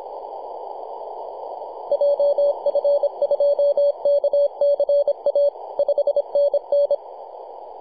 Morse code, heard as a keyed tone, about 25 wpm, 570 Hz. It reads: JF2KCA 5NN
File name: 2229z 4W/JI1AVY 21.023 CW